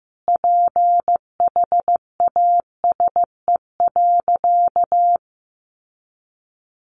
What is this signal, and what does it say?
Morse code, heard as a keyed tone, about 15 wpm, 700 Hz. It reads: PHASE.
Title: Sentences from Top 1200 Words 15wpm